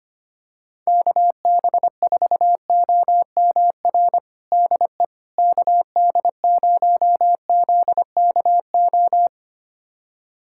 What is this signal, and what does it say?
Morse code, heard as a keyed tone, about 25 wpm, 700 Hz. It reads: KB4OMR DE KD0ZKO
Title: USA Callsign DE Callsign - 6 Characters - 25wpm